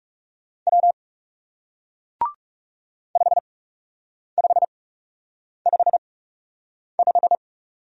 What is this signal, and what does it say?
Morse code, heard as a keyed tone, about 37 wpm, spaced as for 8 wpm, 700 Hz. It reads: W 5555
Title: Speed-Racing Single Letter-Number 30wpm